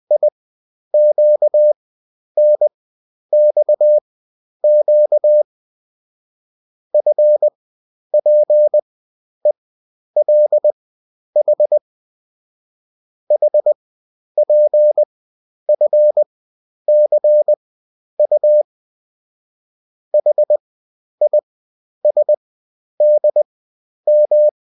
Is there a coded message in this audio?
Yes: Morse code from 0.1 to 24.5 s, heard as a keyed tone, about 20 wpm, spaced as for 10 wpm, 600 Hz. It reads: IQNXQ FPELH HPFCU HISDM